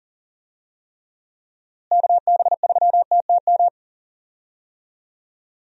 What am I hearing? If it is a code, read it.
Morse code, heard as a keyed tone, about 40 wpm, 700 Hz. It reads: KB3TTM
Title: USA Call Signs - 6 Characters - Repeated 3x - 40wpm